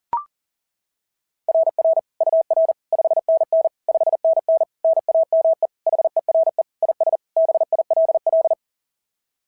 Morse code, heard as a keyed tone, about 37 wpm, 660 Hz. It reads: RR UR 5NN 5NN NAME HERE IS BILL
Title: SOTA QSO Practice 40wpm